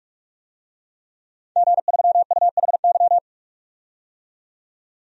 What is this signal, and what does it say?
Morse code, heard as a keyed tone, about 46 wpm, 700 Hz. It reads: G3UHY
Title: DXCC-100 Call Signs - 5 Characters - 45wpm